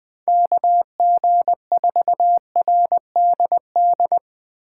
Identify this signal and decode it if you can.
Morse code, heard as a keyed tone, about 20 wpm, 700 Hz. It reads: KG4RDD
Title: USA Call Signs - 6 Characters - 6x Character Spacing - 20wpm